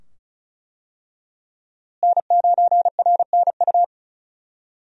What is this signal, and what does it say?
Morse code, heard as a keyed tone, about 34 wpm, 700 Hz. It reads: N9RNU
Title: USA Call Signs - 5 Characters - Repeated 2x - 35wpm